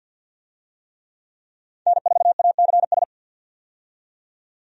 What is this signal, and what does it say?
Morse code, heard as a keyed tone, about 48 wpm, 700 Hz. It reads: N4ACS
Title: USA Call Signs - 5 Characters - 50wpm